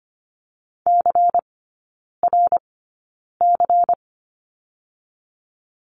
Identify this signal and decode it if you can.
Morse code, heard as a keyed tone, about 25 wpm, spaced as for 9 wpm, 700 Hz. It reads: CRC
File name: course / 25:10wpm / 100 Lesson 17e - Letter u - Three Characters 25wpm